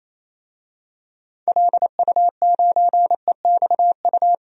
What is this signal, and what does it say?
Morse code, heard as a keyed tone, about 27 wpm, 700 Hz. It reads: LU9EXU
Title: DXCC-100 Call Signs - 6 Characters - Repeated 3x - 28wpm